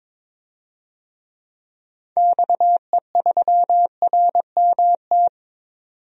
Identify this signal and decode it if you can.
Morse code, heard as a keyed tone, about 22 wpm, 700 Hz. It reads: XE3RMT
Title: DXCC-100 Call Signs - 6 Characters - 22wpm